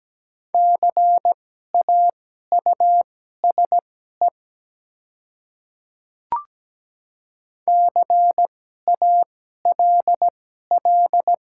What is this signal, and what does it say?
Morse code, heard as a keyed tone, about 17 wpm, spaced as for 12 wpm, 700 Hz. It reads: CAUSE CALL
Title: 2x Character Spacing - Top 300 Words - 17wpm